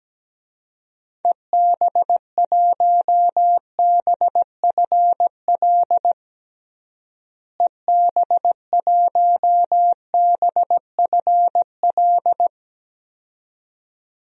Morse code, heard as a keyed tone, about 17 wpm, 700 Hz. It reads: EB1BFL EB1BFL
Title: DXCC-100 Call Signs - 6 Characters - Repeated 2x - 17wpm